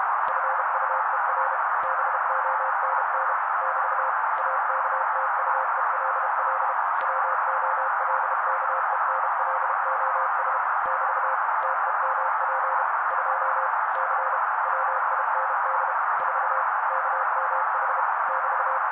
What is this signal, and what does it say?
Morse code, heard as a keyed tone, about 32 wpm, 530 Hz. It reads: FUER BONN <BT> AKTUELL WOLKENLOS <BT> TEMP 2 C WIND 4 KMH <BT>
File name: daily / 31wpm noise